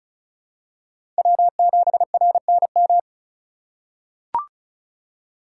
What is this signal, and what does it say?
Morse code, heard as a keyed tone, about 34 wpm, 700 Hz. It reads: W7RNM
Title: USA Call Signs - 5 Characters - 3x Character Spacing - 35wpm